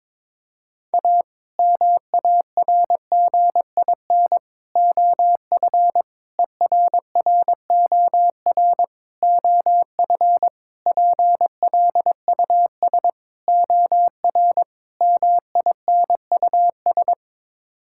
Morse code, heard as a keyed tone, about 22 wpm, 700 Hz. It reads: A MARGIN OF ERROR OF PLUS OR MINUS
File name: practice / 22wpm / Sets of 8 Words - 22wpm